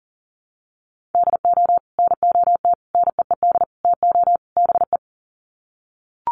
Morse code, heard as a keyed tone, about 40 wpm, 700 Hz. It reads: DO NOT NEED TO BE
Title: Sets of 5 Words Encore 40wpm